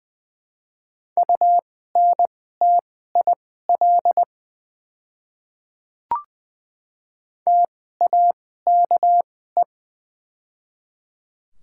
Morse code, heard as a keyed tone, about 20 wpm, spaced as for 14 wpm, 700 Hz. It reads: UNTIL TAKE
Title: 2x Character Spacing - Top 300 Words - 20wpm